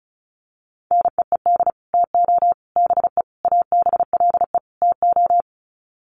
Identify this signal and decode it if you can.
Morse code, heard as a keyed tone, about 34 wpm, 700 Hz. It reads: NEED TO BE ABLE TO
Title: Sets of 5 Words Encore 35wpm